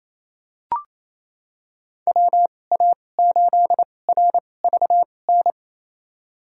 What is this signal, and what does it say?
Morse code, heard as a keyed tone, about 28 wpm, spaced as for 20 wpm, 700 Hz. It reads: WA8RVN
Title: USA Call Signs - 2x Character Spacing - 28wpm